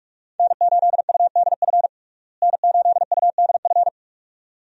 Morse code, heard as a keyed tone, about 46 wpm, 700 Hz. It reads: N8UDF N8UDF
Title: USA Call Signs - 5 Characters - Repeated 2x - 45wpm